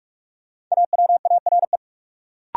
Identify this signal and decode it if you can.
Morse code, heard as a keyed tone, about 46 wpm, 700 Hz. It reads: AWARE
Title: Sentences from Top 1200 Words - New Word Drill - 45wpm